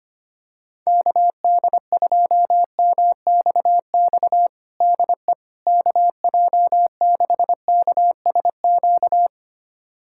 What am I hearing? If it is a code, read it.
Morse code, heard as a keyed tone, about 25 wpm, 700 Hz. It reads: KD2MXX DE KJ6KSQ
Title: USA Callsign DE Callsign 25wpm